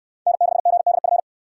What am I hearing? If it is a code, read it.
Morse code, heard as a keyed tone, about 61 wpm, 700 Hz. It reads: N6GDF